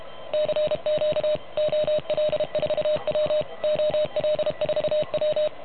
Morse code, heard as a keyed tone, about 31 wpm, 610 Hz. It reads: CQ OL4W OL4W
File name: truSDX / MWC29.8